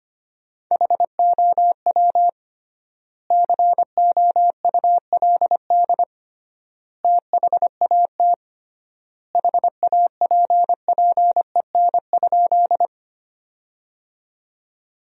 Morse code, heard as a keyed tone, about 25 wpm, 700 Hz. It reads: HOW COULD THAT HAPPEN?
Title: 3x Word-Spacing - Sentences from Top 400 Words - 25wpm